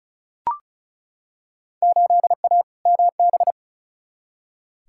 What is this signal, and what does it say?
Morse code, heard as a keyed tone, about 34 wpm, 700 Hz. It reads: 8A MB
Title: Field Day Exchanges Encore - 35wpm